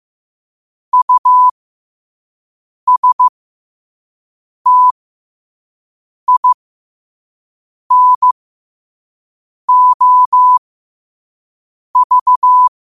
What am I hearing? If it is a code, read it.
Morse code, heard as a keyed tone, about 15 wpm, spaced as for 5 wpm, 1000 Hz. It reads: USTINOV
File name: qotdfiles / 5wpm